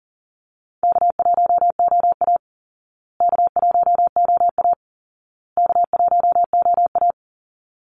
Morse code, heard as a keyed tone, about 40 wpm, 700 Hz. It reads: K1OA K1OA K1OA
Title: Call Signs Repeated 2x - 40wpm